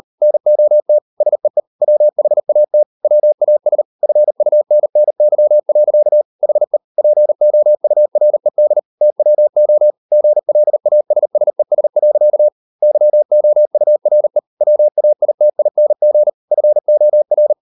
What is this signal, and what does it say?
Morse code, heard as a keyed tone, about 38 wpm, 600 Hz. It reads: NOT SEE WHAT WAS FUNNY. HE POURED TWO GLASSES. YOURE WAITING FOR